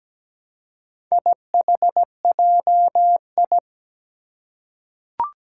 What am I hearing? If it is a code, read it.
Morse code, heard as a keyed tone, about 17 wpm, 700 Hz. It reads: IHJI
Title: Four Challenging Letter-Number Combinations 17wpm